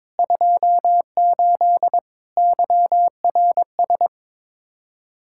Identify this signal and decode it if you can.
Morse code, heard as a keyed tone, about 22 wpm, 700 Hz. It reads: 28 YRS
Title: K1USN SST QSO Practice 22wpm